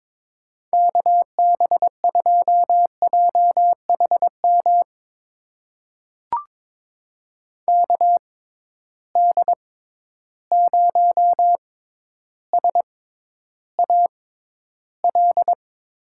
Morse code, heard as a keyed tone, about 22 wpm, spaced as for 11 wpm, 700 Hz. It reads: KB2JHM KD0SAL
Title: USA Call Signs - 6 Characters - 6x Character Spacing - 22wpm